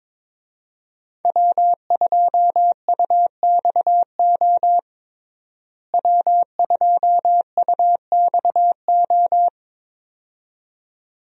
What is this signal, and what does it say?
Morse code, heard as a keyed tone, about 22 wpm, 700 Hz. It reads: W2UXO W2UXO
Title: USA Call Signs - 5 Characters - Repeated 2x - 22wpm